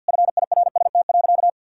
Morse code, heard as a keyed tone, about 50 wpm, 700 Hz. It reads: FIRST.
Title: Sentences from Top 1000 Words - Review Part 6 - 50wpm